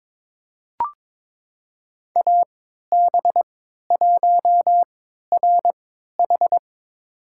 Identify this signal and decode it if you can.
Morse code, heard as a keyed tone, about 22 wpm, spaced as for 13 wpm, 700 Hz. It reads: AB1RH
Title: USA Call Signs - 3x Character Spacing - 22wpm